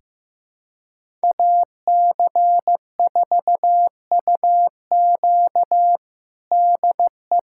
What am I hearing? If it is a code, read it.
Morse code, heard as a keyed tone, about 15 wpm, 700 Hz. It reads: AC4UQ DE
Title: USA Callsign DE Callsign - 5 Characters - 15wpm